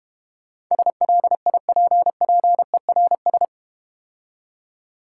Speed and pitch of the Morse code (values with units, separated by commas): 31 wpm, 700 Hz